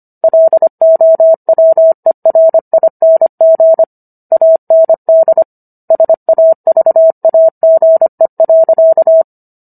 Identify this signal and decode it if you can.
Morse code, heard as a keyed tone, about 25 wpm, 650 Hz. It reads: LOWERING AND SAVAGE.